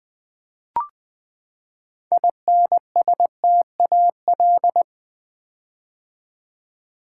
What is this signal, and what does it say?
Morse code, heard as a keyed tone, about 20 wpm, 700 Hz. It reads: INSTAL
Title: Build A Word - 6 Letters At A Time - 12-Letter Words - 20wpm